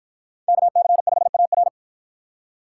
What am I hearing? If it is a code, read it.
Morse code, heard as a keyed tone, about 48 wpm, 700 Hz. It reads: KC5AR